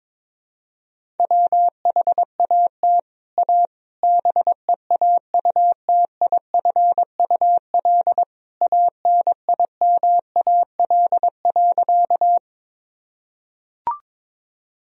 Morse code, heard as a keyed tone, about 23 wpm, 700 Hz. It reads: WHAT A BEAUTIFUL ANIMAL.
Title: Sentences from Top 1000 Words - Review Part 1 - 22wpm